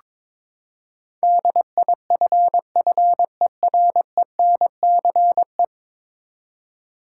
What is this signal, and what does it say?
Morse code, heard as a keyed tone, about 22 wpm, 700 Hz. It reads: DIFFERENCE
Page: morse-code-ninja-practice-22wpm